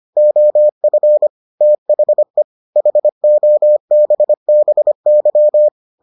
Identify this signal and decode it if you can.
Morse code, heard as a keyed tone, about 25 wpm, 600 Hz. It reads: OF THE HOBBY